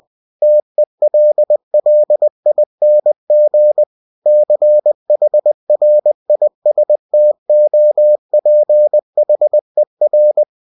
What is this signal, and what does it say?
Morse code, heard as a keyed tone, about 20 wpm, 600 Hz. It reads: TELLING CHRISTOPHER
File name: cw-books / winnie-the-pooh / winnie-20wpm-page-0106